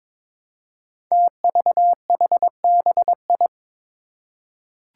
Morse code, heard as a keyed tone, about 22 wpm, 700 Hz. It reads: TVHBI
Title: Five Challenging Letter-Number Combinations 22wpm